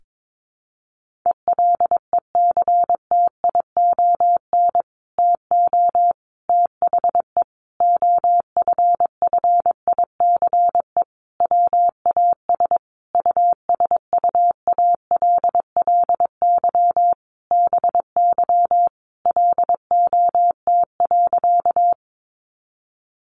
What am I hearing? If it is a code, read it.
Morse code, heard as a keyed tone, about 22 wpm, 700 Hz. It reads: ELECTION TO THE OFFICE WAS USUALLY BY LOT.
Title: Sentences from Top 1500 Words 22wpm